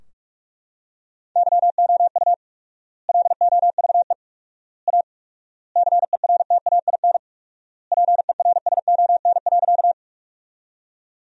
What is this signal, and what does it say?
Morse code, heard as a keyed tone, about 41 wpm, 700 Hz. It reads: YOU LOVE A CERTAIN PERSON.
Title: 4x Word-Spacing - Sentences from Top 500 Words - 45wpm